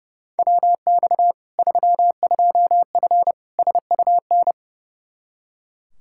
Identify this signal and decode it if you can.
Morse code, heard as a keyed tone, about 30 wpm, 700 Hz. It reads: WX 32F SUN
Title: Practice QSO - Exchange 2 - Info - 30wpm